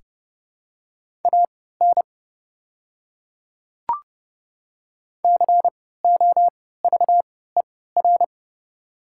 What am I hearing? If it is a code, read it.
Morse code, heard as a keyed tone, about 30 wpm, spaced as for 17 wpm, 690 Hz. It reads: AN COVER
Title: 3x Character Spacing - Top 300 Words - 30wpm